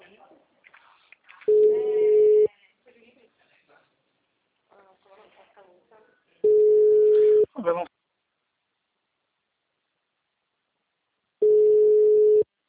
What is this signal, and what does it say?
Ringback tone (Europe, 425 Hz), with speech.